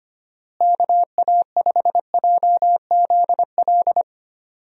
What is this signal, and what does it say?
Morse code, heard as a keyed tone, about 25 wpm, 700 Hz. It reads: KA5JZL
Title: USA Call Signs - 6 Characters - Repeated 2x - 25wpm